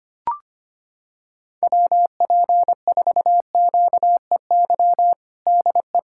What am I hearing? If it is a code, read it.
Morse code, heard as a keyed tone, about 25 wpm, 700 Hz. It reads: WP4QEY DE
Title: USA Callsign DE Callsign 25wpm